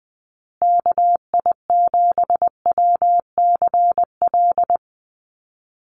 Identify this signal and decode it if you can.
Morse code, heard as a keyed tone, about 20 wpm, 700 Hz. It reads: KI7WCL
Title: USA Call Signs Repeated 3x 20wpm